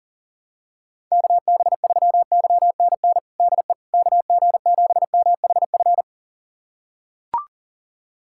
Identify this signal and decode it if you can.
Morse code, heard as a keyed tone, about 40 wpm, 700 Hz. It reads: KB3YNN DE KG7MHF
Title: USA Callsign DE Callsign - 6 Characters - 40wpm